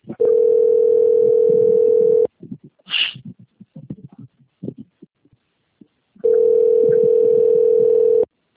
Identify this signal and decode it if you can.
Ringback tone (North America), 440 + 480 Hz.